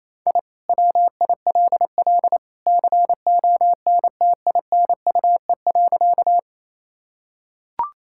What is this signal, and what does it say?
Morse code, heard as a keyed tone, about 28 wpm, 700 Hz. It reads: I WILL CONTINUE.
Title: Sentences from Top 1000 Words - Review Part 6 - 28wpm